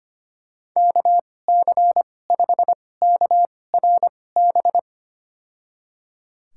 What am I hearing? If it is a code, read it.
Morse code, heard as a keyed tone, about 25 wpm, spaced as for 18 wpm, 700 Hz. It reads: KC5KRB
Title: USA Call Signs - 6 Characters - 2x Character Spacing - 25wpm